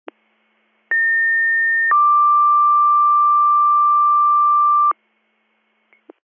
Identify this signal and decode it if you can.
Two-tone page, heard as a one-second tone, then a three-second tone, about 1800 then 1180 Hz. It is heard from 0.9 to 4.9 s.